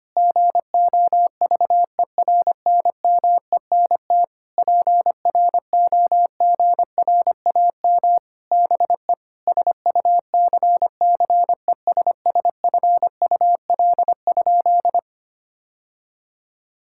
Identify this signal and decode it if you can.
Morse code, heard as a keyed tone, about 25 wpm, 700 Hz. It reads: GOVERNMENT PROGRAM BE SUCCESSFUL?
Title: Sentences from Top 1000 Words - Review Part 5 - 25wpm